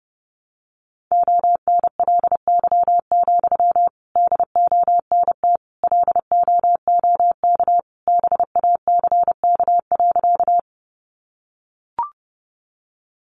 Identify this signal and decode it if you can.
Morse code, heard as a keyed tone, about 30 wpm, 700 Hz. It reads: ONLY, DONT LOOK BACK.
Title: Sentences from Top 300 Words 30wpm